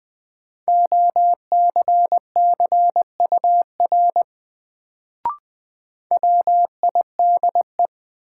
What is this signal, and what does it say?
Morse code, heard as a keyed tone, about 20 wpm, 700 Hz. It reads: OCCUR WIDE
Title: Sentences from Top 900 Words - New Word Drill - 20wpm